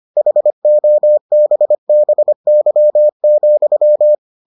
Morse code, heard as a keyed tone, about 25 wpm, 600 Hz. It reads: HOBBY,